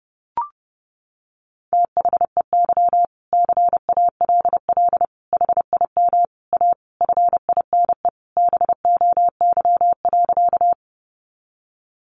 Morse code, heard as a keyed tone, about 28 wpm, 700 Hz. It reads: THEY CALL HIM A FINE BOY.
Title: Sentences from Top 500 Words 30wpm